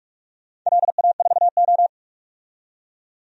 Morse code, heard as a keyed tone, about 44 wpm, 700 Hz. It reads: LA4O
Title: DXCC-100 Call Signs - 4 Characters - 45wpm